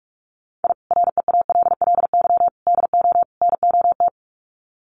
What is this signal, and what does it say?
Morse code, heard as a keyed tone, about 44 wpm, 700 Hz. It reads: I REALLY DO NOT
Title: Sets of 4 Words 45wpm